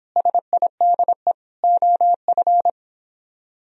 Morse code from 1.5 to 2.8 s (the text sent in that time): OF